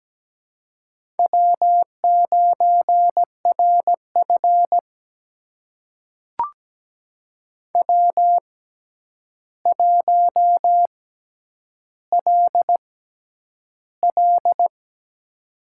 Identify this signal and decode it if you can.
Morse code, heard as a keyed tone, about 17 wpm, spaced as for 9 wpm, 700 Hz. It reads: W9RF W1LL
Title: USA Call Signs - 4 Characters - 6x Character Spacing - 17wpm